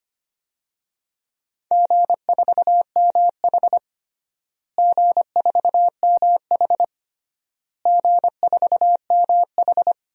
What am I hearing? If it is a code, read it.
Morse code, heard as a keyed tone, about 25 wpm, 700 Hz. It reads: G4MH G4MH G4MH
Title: DXCC-100 Call Signs - 4 Characters - Repeated 3x - 25wpm